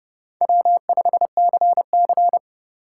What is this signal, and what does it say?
Morse code, heard as a keyed tone, about 30 wpm, 700 Hz. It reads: W5CC
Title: USA Call Signs - 4 Characters - 2x Character Spacing - 30wpm